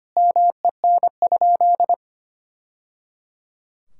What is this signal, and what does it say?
Morse code, heard as a keyed tone, about 25 wpm, 700 Hz. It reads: MEN?
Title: Sentences from Top 1000 Words - Review Part 6 - 25wpm